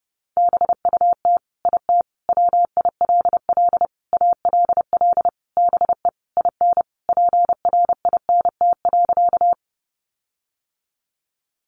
Morse code, heard as a keyed tone, about 30 wpm, 700 Hz. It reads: BUT IT WILL ALL BE IN PRINT.
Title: Sentences from Top 1400 Words 30wpm